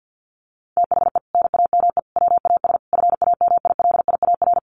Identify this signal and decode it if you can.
Morse code, heard as a keyed tone, about 50 wpm, 700 Hz. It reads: THE NAME WAS FAMILIAR
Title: Sentences from Top 1000 Words 50wpm